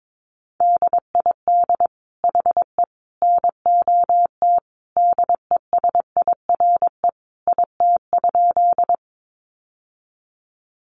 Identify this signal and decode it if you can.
Morse code, heard as a keyed tone, about 22 wpm, 700 Hz. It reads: DID HE NOT DESIRE IT?
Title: Sentences from Top 1100 Words 22wpm